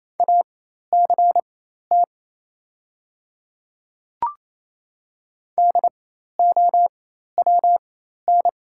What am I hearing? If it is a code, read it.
Morse code, heard as a keyed tone, about 28 wpm, spaced as for 13 wpm, 700 Hz. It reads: ACT DOWN
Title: 4x Character Spacing - Top 200 Words - 28wpm